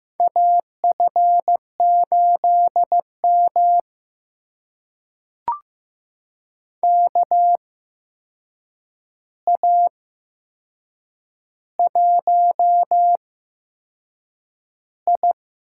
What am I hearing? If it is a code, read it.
Morse code, heard as a keyed tone, about 15 wpm, spaced as for 6 wpm, 700 Hz. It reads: AF8M KA1I
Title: USA Call Signs - 4 Characters - 8x Character Spacing - 15wpm